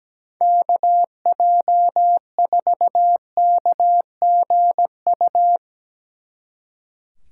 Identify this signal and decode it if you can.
Morse code, heard as a keyed tone, about 18 wpm, 700 Hz. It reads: KJ4KGU
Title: Speed Racing - USA Call Signs 17wpm